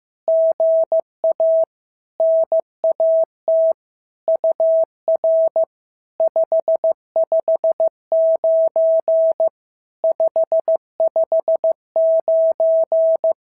Morse code, heard as a keyed tone, about 15 wpm, 650 Hz. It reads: GA NAT UR 559 559
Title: SOTA QSO Practice 15wpm